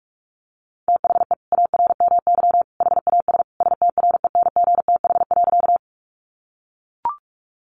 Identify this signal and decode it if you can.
Morse code, heard as a keyed tone, about 44 wpm, 700 Hz. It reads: THE ARMY HAS STRENGTH.
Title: Sentences from Top 1300 Words 45wpm